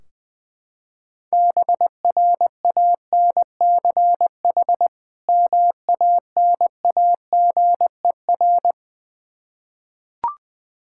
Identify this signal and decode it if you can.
Morse code, heard as a keyed tone, about 20 wpm, 700 Hz. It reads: BRANCH MANAGER
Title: Practice QSO - Exchange 3 - More Info - 20wpm